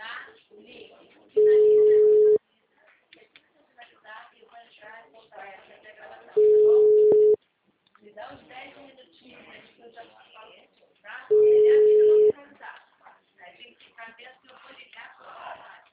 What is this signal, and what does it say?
Ringback tone (Europe, 425 Hz), with speech.